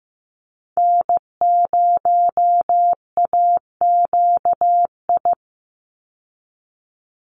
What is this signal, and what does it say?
Morse code, heard as a keyed tone, about 15 wpm, 700 Hz. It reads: N0AQI
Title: USA Call Signs 15wpm